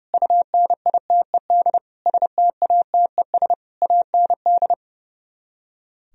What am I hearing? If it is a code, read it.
Morse code, heard as a keyed tone, about 36 wpm, 700 Hz. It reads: UNITED STATES AND
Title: Speed-Racing - Sets of 5 Words 30wpm